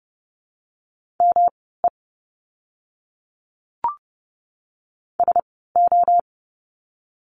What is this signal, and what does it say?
Morse code, heard as a keyed tone, about 30 wpm, spaced as for 17 wpm, 700 Hz. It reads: ME SO
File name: course / 30:10wpm / 230 Lesson 37f - Letter x - Words 30wpm